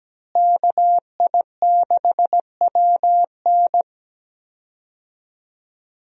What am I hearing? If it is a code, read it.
Morse code, heard as a keyed tone, about 17 wpm, 700 Hz. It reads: KI6WN
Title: USA Call Signs - 5 Characters - Repeated 2x - 17wpm